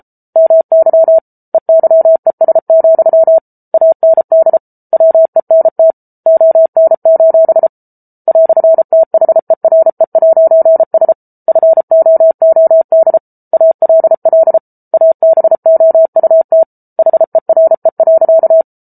Morse code, heard as a keyed tone, about 34 wpm, 650 Hz. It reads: MY EYES, AND WENT ON: "THERE'S FOOD ALL ABOUT HERE.